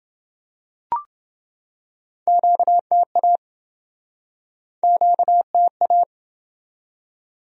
Morse code, heard as a keyed tone, about 26 wpm, 700 Hz. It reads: QTA QTA